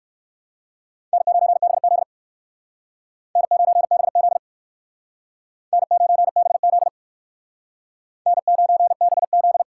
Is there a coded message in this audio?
Yes: Morse code from 1.1 to 9.6 s, heard as a keyed tone, about 56 wpm, 700 Hz. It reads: N9BZ N9BZ N9BZ N9BZ